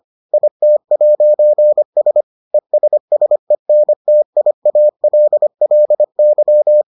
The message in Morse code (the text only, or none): IT'S ESSENTIALLY